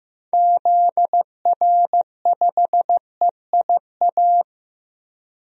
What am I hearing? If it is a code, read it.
Morse code, heard as a keyed tone, about 15 wpm, 700 Hz. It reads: ZR5EIA
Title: DXCC-100 Call Signs - 6 Characters - 15wpm